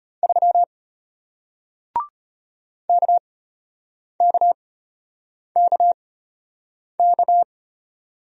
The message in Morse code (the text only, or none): 3 KKKK